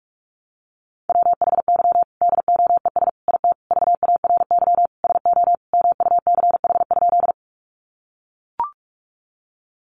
Morse code, heard as a keyed tone, about 44 wpm, 700 Hz. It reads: WHY DOES IT VARY SO MUCH?
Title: Sentences from Top 1200 Words 45wpm